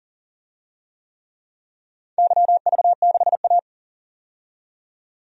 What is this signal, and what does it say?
Morse code, heard as a keyed tone, about 40 wpm, 700 Hz. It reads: YV6A
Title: DXCC-100 Call Signs - 4 Characters - 40wpm